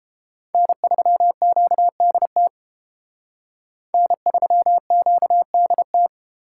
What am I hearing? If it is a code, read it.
Morse code, heard as a keyed tone, about 30 wpm, 700 Hz. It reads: N3QDT N3QDT